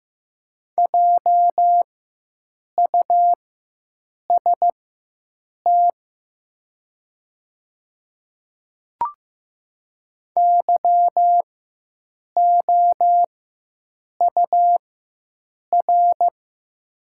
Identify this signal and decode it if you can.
Morse code, heard as a keyed tone, about 15 wpm, spaced as for 7 wpm, 700 Hz. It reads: JUST YOUR
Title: 4x Character Spacing - Top 300 Words - 15wpm